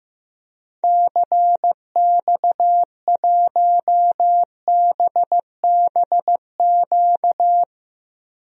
Morse code, heard as a keyed tone, about 15 wpm, 700 Hz. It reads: CX1BBQ
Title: DXCC-100 Call Signs - 6 Characters - 15wpm